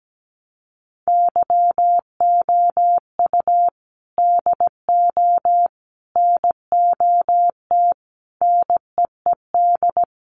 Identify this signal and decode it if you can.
Morse code, heard as a keyed tone, about 17 wpm, 700 Hz. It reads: YOU DO NOT NEED
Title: Sets of 4 Words 17wpm